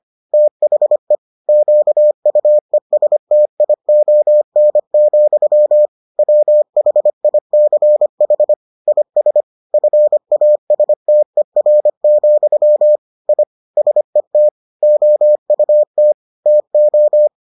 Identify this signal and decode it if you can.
Morse code, heard as a keyed tone, about 25 wpm, 600 Hz. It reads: THE QUESTION, WHICH IS FASTER, I SET OUT TO